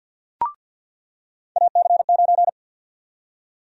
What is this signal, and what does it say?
Morse code, heard as a keyed tone, about 53 wpm, 700 Hz. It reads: AC9
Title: USA Call Sign Prefixes - 50wpm